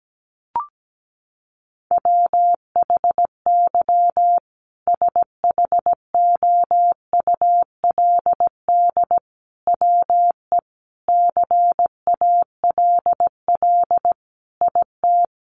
Morse code, heard as a keyed tone, about 17 wpm, 700 Hz. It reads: WHY SHOULD WE CALL IT
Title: Sentences from Top 1100 Words 17wpm